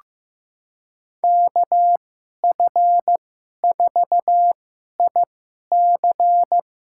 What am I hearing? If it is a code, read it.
Morse code, heard as a keyed tone, about 15 wpm, spaced as for 11 wpm, 700 Hz. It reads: KF4IC
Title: USA Call Signs - 5 Characters - 2x Character Spacing - 15wpm